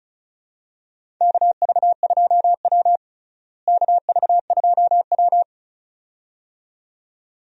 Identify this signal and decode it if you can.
Morse code, heard as a keyed tone, about 34 wpm, 700 Hz. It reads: KV2W KV2W
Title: USA Call Signs - 4 Characters - Repeated 2x - 35wpm